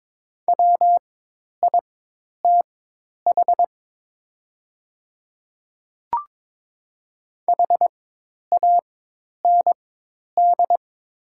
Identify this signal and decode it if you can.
Morse code, heard as a keyed tone, about 22 wpm, spaced as for 10 wpm, 700 Hz. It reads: WITH HAND
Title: 4x Character Spacing - Top 200 Words - 22wpm